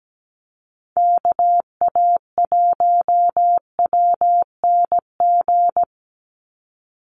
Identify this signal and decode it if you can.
Morse code, heard as a keyed tone, about 17 wpm, 700 Hz. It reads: KA1WNG
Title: USA Call Signs 17wpm